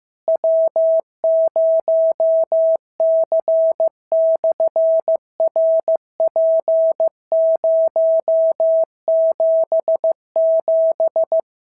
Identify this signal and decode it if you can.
Morse code, heard as a keyed tone, about 15 wpm, 650 Hz. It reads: W0C/RP077
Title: SOTA QSO Practice 15wpm